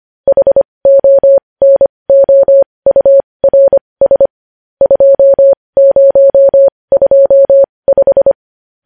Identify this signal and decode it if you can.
Morse code, heard as a keyed tone, about 25 wpm, spaced as for 20 wpm, 550 Hz. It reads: HONOURS 2025
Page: News Headlines in Morse Code at 20 WPM